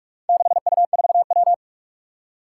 Morse code, heard as a keyed tone, about 42 wpm, 700 Hz. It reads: 6U4W
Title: Four Challenging Letter-Number Combinations 45wpm